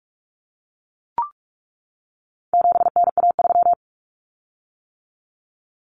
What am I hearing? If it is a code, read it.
Morse code, heard as a keyed tone, about 44 wpm, 700 Hz. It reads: 7NA3
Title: Common Cut Numbers - 4-Digit 45wpm